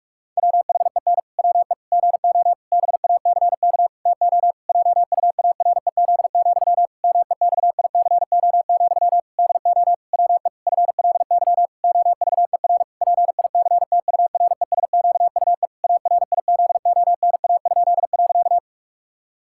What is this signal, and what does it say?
Morse code, heard as a keyed tone, about 46 wpm, 700 Hz. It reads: WHEN WE GO BACK TO JUAREZ, MEXICO, DO WE FLY OVER PICTURESQUE ARIZONA?.
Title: Pangram Challenge v2 - 45wpm